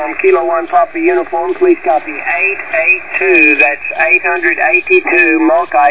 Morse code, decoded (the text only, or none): none